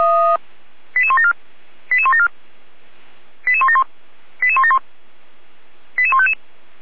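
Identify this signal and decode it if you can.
Five-tone selective call (ZVEI), heard as quick runs of five tones, tones of about 70 ms. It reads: 88174 88174 88171 88171 88155
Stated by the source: ZVEI five-tone selective call received by SDR.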